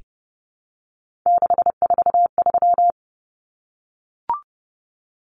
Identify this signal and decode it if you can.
Morse code, heard as a keyed tone, about 30 wpm, 700 Hz. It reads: 643
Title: Common Cut Numbers - 3-Digit 30wpm